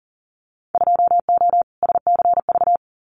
Morse code, heard as a keyed tone, about 40 wpm, 700 Hz. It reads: 2O SCV
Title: Winter Field Day Exchanges Encore 40wpm